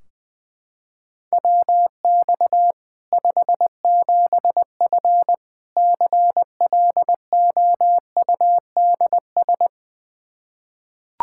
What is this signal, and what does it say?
Morse code, heard as a keyed tone, about 20 wpm, 700 Hz. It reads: WX 57F CLOUDS
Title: K1USN SST QSO Practice 20wpm